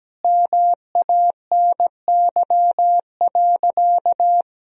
Morse code, heard as a keyed tone, about 17 wpm, 700 Hz. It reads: MANY.
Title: Sentences from Top 1000 Words - Review Part 1 - 17wpm